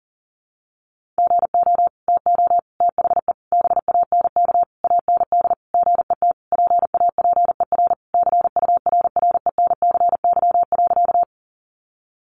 Morse code, heard as a keyed tone, about 40 wpm, 700 Hz. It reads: GO TO THE BANK AND GET PAPER CURRENCY.
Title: Sentences from Top 1500 Words 40wpm